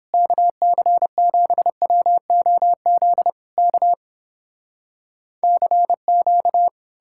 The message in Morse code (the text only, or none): KC7WOZ K CQ